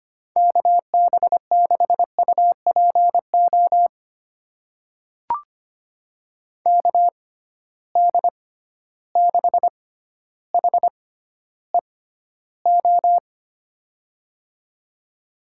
Morse code, heard as a keyed tone, about 25 wpm, spaced as for 13 wpm, 700 Hz. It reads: KB6UPO KD6HEO
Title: USA Call Signs - 6 Characters - 6x Character Spacing - 25wpm